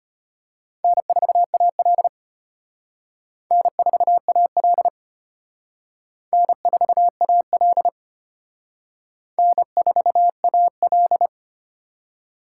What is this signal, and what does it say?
Morse code, heard as a keyed tone, about 31 wpm, 700 Hz. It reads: N4AL N4AL N4AL N4AL
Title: Speed Racing - USA 1x2 Call Signs 25wpm